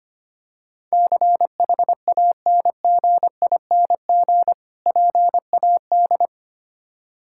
Morse code, heard as a keyed tone, about 25 wpm, 700 Hz. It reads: CHANGING PAD
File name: practice / 25wpm / Difficult Words with Hints - 25wpm